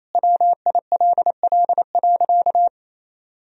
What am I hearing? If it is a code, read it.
Morse code, heard as a keyed tone, about 28 wpm, 700 Hz. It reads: WILL.
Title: Sentences from Top 1000 Words - Review Part 1 - 28wpm